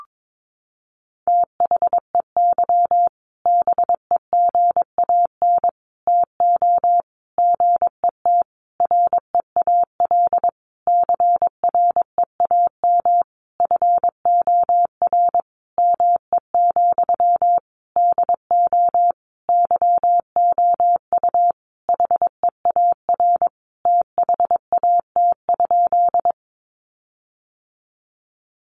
Morse code, heard as a keyed tone, about 22 wpm, 700 Hz. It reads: THEY BEGAN TO GET REAL CREAM FOR ME, DO YOU HEAR THAT?
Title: Sentences from Top 1500 Words 22wpm